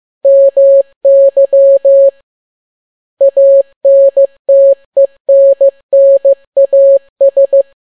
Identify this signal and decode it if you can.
Morse code, heard as a keyed tone, about 15 wpm, 550 Hz. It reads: MY ANTENNAS
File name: n6tt p1 15C13